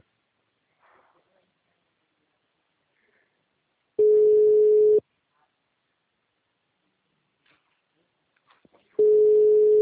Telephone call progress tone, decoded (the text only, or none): ringback tone